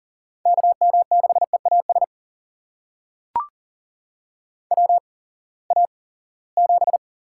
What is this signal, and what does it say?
Morse code, heard as a keyed tone, about 40 wpm, spaced as for 23 wpm, 700 Hz. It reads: KM6EAS WA7
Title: USA Call Signs - 6 Characters - 8x Character Spacing - 40wpm